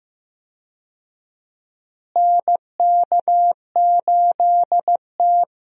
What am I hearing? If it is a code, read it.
Morse code, heard as a keyed tone, about 15 wpm, 700 Hz. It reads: NK8T
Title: USA Call Signs - 4 Characters - Repeated 3x - 15wpm